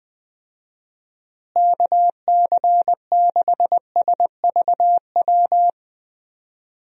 Morse code, heard as a keyed tone, about 20 wpm, 700 Hz. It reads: KC6SVW
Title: USA Call Signs - 6 Characters - Repeated 2x - 20wpm